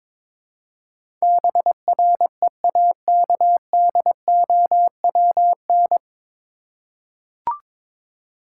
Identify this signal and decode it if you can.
Morse code, heard as a keyed tone, about 22 wpm, 700 Hz. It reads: BREAKDOWN
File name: 9 Letter Words - First 3 Letters Given - 22wpm